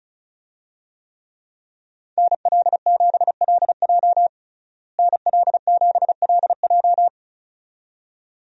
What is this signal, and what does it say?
Morse code, heard as a keyed tone, about 34 wpm, 700 Hz. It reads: NL7LJ NL7LJ
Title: USA Call Signs - 5 Characters - Repeated 2x - 35wpm